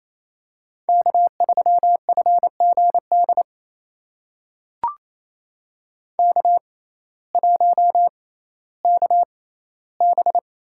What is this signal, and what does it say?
Morse code, heard as a keyed tone, about 28 wpm, spaced as for 15 wpm, 700 Hz. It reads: K3FGD K1KB
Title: USA Call Signs - 6x Character Spacing - 28wpm